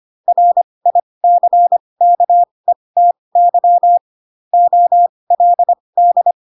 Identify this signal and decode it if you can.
Morse code, heard as a keyed tone, about 25 wpm, spaced as for 20 wpm, 700 Hz. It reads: RICKETY OLD